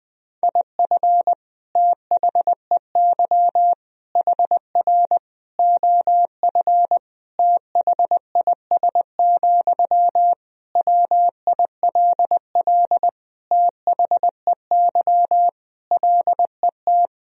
Morse code, heard as a keyed tone, about 20 wpm, 700 Hz. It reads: IF THEY HR OF THIS, WILL THEY LET
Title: Sentences from Top 400 Words and Abbreviations 20wpm